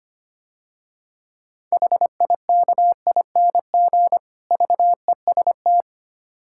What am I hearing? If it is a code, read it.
Morse code, heard as a keyed tone, about 25 wpm, 690 Hz. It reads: HIKING VEST
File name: practice / 25wpm / Difficult Words - Speed-Racing with Hints - 25wpm